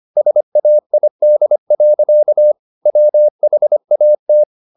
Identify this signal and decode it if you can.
Morse code, heard as a keyed tone, about 25 wpm, 600 Hz. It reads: SAID. WHAT
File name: the old man and the sea part 1 25wpm